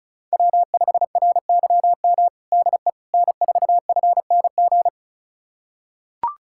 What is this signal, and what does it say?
Morse code, heard as a keyed tone, about 36 wpm, 700 Hz. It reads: W5RYM DE N4FNG
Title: USA Callsign DE Callsign - 5 Characters - 35wpm